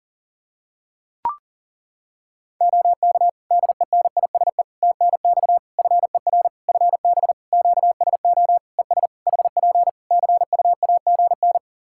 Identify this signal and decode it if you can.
Morse code, heard as a keyed tone, about 37 wpm, 700 Hz. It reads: OK DENISE TNX FER FB QSO ES HP CUAGN
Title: Practice QSO - Exchange 4 - Ending - 40wpm